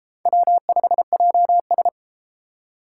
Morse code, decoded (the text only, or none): W5JS